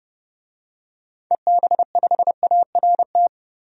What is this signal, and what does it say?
Morse code, heard as a keyed tone, about 30 wpm, 700 Hz. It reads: EB5ART
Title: DXCC-100 Call Signs - 6 Characters - 30wpm